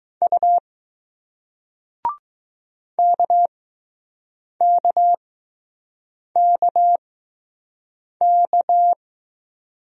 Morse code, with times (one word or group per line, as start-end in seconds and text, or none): 0.2-0.6 U
3.0-8.9 KKKK